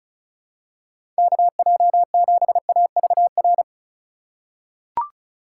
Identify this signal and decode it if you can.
Morse code, heard as a keyed tone, about 34 wpm, 700 Hz. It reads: KJ7AVR
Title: USA Call Signs - 6 Characters - 4x Character Spacing - 35wpm